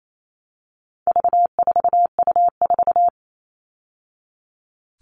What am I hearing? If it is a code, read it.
Morse code, heard as a keyed tone, about 28 wpm, 700 Hz. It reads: V4U4
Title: All Cut Numbers - 4-Digit 28wpm